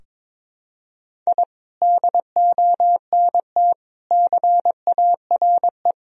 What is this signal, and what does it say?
Morse code, heard as a keyed tone, about 22 wpm, 700 Hz. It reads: I DONT CARE
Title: Sentences from Top 1000 Words - Review Part 3 - 22wpm